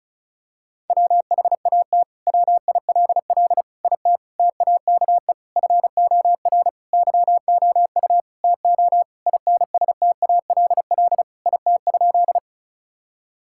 Morse code, heard as a keyed tone, about 36 wpm, 700 Hz. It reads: WHAT WILL IT TAKE FOR YOU TO INSTALL IT?
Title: Sentences from Top 1600 Words 35wpm